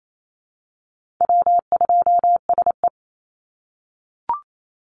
Morse code, heard as a keyed tone, about 28 wpm, 700 Hz. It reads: W2SE
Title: NA SOTA Chaser Callsigns 28wpm